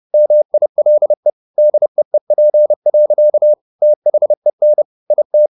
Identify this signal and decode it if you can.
Morse code, heard as a keyed tone, about 30 wpm, 600 Hz. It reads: MILE DEEP. THEN IT